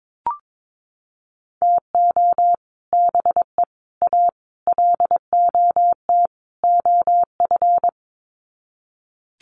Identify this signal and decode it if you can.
Morse code, heard as a keyed tone, about 22 wpm, 700 Hz. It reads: TO BE A LOT OF
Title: Sets of 5 Words Encore 22wpm